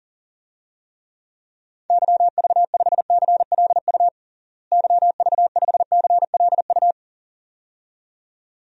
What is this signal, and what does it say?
Morse code, heard as a keyed tone, about 37 wpm, 700 Hz. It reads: YV5CLU YV5CLU
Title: DXCC-100 Call Signs - 6 Characters - Repeated 2x - 40wpm